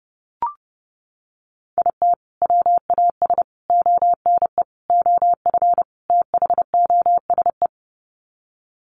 Morse code, heard as a keyed tone, about 30 wpm, 700 Hz. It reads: IT WAS ONE OF THOSE
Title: Sets of 5 Words Encore 30wpm